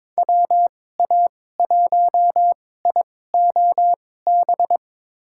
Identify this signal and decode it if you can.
Morse code, heard as a keyed tone, about 22 wpm, spaced as for 16 wpm, 700 Hz. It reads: WA1IOB
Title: USA Call Signs - 6 Characters - 2x Character Spacing - 22wpm